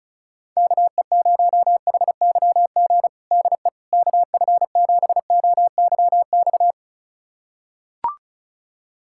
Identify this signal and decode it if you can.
Morse code, heard as a keyed tone, about 36 wpm, 700 Hz. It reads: KE0HYG DE KF7OYX
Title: USA Callsign DE Callsign - 6 Characters - 35wpm